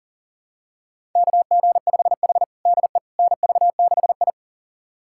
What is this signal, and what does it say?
Morse code, heard as a keyed tone, about 40 wpm, 700 Hz. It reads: KG5H DE NV6I
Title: USA Callsign DE Callsign - 4 Characters - 40wpm